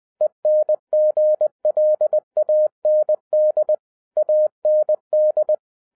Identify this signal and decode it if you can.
Morse code, heard as a keyed tone, about 20 wpm, 610 Hz. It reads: ENGLAND AND